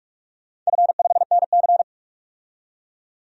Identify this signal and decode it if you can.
Morse code, heard as a keyed tone, about 47 wpm, 700 Hz. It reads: F5NC
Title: DXCC-100 Call Signs - 4 Characters - 45wpm